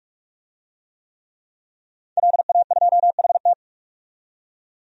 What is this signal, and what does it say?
Morse code, heard as a keyed tone, about 46 wpm, 700 Hz. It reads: LA2HT
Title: DXCC-100 Call Signs - 5 Characters - 45wpm